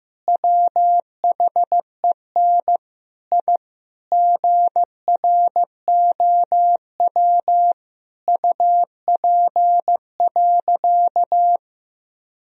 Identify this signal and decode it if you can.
Morse code, heard as a keyed tone, about 15 wpm, 700 Hz. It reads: WHEN I GROW UP.
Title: Sentences from Top 1000 Words - Review Part 1 - 15wpm